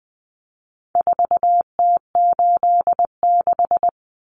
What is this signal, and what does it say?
Morse code, heard as a keyed tone, about 20 wpm, 700 Hz. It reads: 4T86
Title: Common Cut Numbers - 4-Digit 20wpm